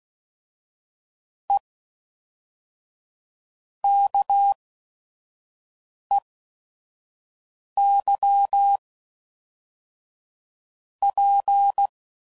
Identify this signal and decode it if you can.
Morse code, heard as a keyed tone, about 16 wpm, spaced as for 5 wpm, 800 Hz. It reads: E KEY P
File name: CW Code Groups @5WPM / Technical Subjects / Reactance